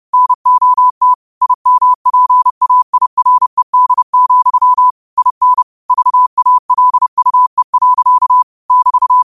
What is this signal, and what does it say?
Morse code, heard as a keyed tone, about 30 wpm, 1000 Hz. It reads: NOT IMPAIRED, IN VALUE. <BT>